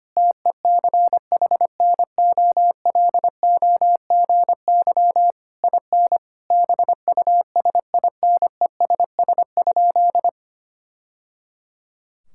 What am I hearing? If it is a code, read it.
Morse code, heard as a keyed tone, about 25 wpm, 700 Hz. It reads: TECHNOLOGY IN BUSINESS?
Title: Sentences from Top 1000 Words - Review Part 5 - 25wpm